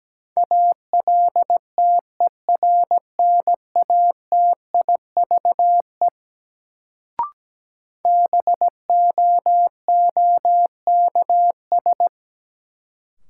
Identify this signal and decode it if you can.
Morse code, heard as a keyed tone, about 17 wpm, 700 Hz. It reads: ALTERNATIVE BOOKS
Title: Sentences from Top 1000 Words - Review Part 4 - New Word Drill - 17wpm